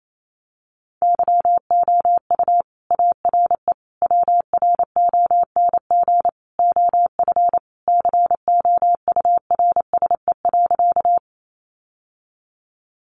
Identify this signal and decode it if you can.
Morse code, heard as a keyed tone, about 28 wpm, 700 Hz. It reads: YOU ARE WRONG OF COURSE.
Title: Sentences from Top 900 Words 28wpm